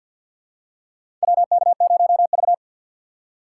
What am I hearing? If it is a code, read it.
Morse code, heard as a keyed tone, about 48 wpm, 700 Hz. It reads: WK0V
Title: USA Call Signs - 4 Characters - 3x Character Spacing - 50wpm